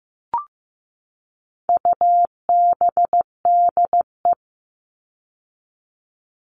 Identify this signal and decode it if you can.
Morse code, heard as a keyed tone, about 15 wpm, 700 Hz. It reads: UBDE
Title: All Cut Numbers - 4-Digit 15wpm